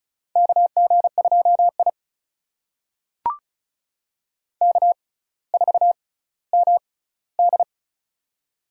Morse code, heard as a keyed tone, about 36 wpm, spaced as for 18 wpm, 700 Hz. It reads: KG2I K4MD
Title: USA Call Signs - 4 Characters - 6x Character Spacing - 35wpm